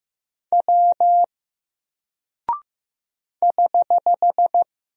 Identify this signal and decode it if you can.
Morse code, heard as a keyed tone, about 15 wpm, 700 Hz. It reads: W <HH>
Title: Single Character (Letter:Number:Punctuation:Prosign) Alt 15wpm